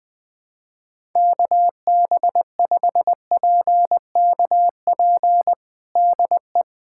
Morse code, heard as a keyed tone, about 20 wpm, 700 Hz. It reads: KB5PKP DE
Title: USA Callsign DE Callsign - 6 Characters - 20wpm